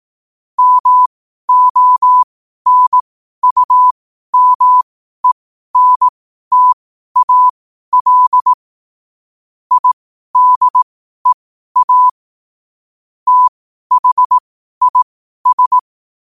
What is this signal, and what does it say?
Morse code, heard as a keyed tone, about 18 wpm, spaced as for 12 wpm, 1000 Hz. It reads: MONUMENTAL IDEA THIS
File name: qotdfiles / 12wpm